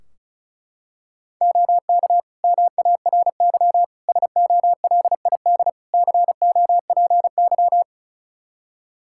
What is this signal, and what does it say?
Morse code, heard as a keyed tone, about 34 wpm, 700 Hz. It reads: OK MARY SOLID COPY
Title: Practice QSO - Exchange 3 - More Info - 35wpm